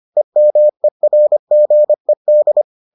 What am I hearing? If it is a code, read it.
Morse code, heard as a keyed tone, about 25 wpm, 600 Hz. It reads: EMERGED